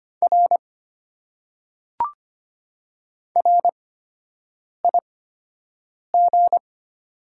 Morse code, heard as a keyed tone, about 25 wpm, spaced as for 7 wpm, 700 Hz. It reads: R RIG
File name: Top 90 QSO Elements - 8x Character Spacing - 25wpm